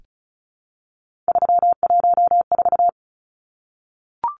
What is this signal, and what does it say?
Morse code, heard as a keyed tone, about 34 wpm, 700 Hz. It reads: 314
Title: Numbers - 3-Digit 35wpm